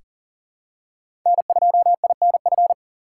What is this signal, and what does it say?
Morse code, heard as a keyed tone, about 40 wpm, 700 Hz. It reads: N2INF
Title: USA Call Signs - 3x Character Spacing - 40wpm